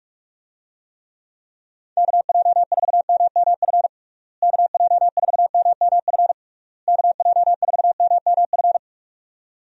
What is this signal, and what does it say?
Morse code, heard as a keyed tone, about 46 wpm, 700 Hz. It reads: KJ4MMF KJ4MMF KJ4MMF
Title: USA Call Signs - 6 Characters - Repeated 3x - 45wpm